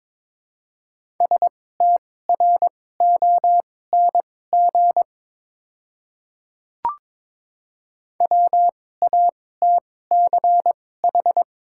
Morse code, heard as a keyed tone, about 22 wpm, spaced as for 16 wpm, 700 Hz. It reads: STRONG WATCH
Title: 2x Character Spacing - Top 500 Words - 22wpm